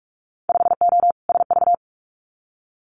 Morse code, heard as a keyed tone, about 46 wpm, 700 Hz. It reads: 5O SV
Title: Winter Field Day Exchanges Encore 45wpm